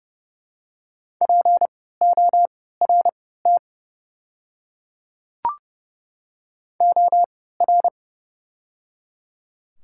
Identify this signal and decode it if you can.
Morse code, heard as a keyed tone, about 30 wpm, spaced as for 17 wpm, 700 Hz. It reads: PORT OR
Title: 3x Character Spacing - Top 200 Words - 30wpm